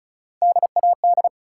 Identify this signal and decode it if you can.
Morse code, heard as a keyed tone, about 36 wpm, 700 Hz. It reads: DAD
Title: Three Letter Words - 35wpm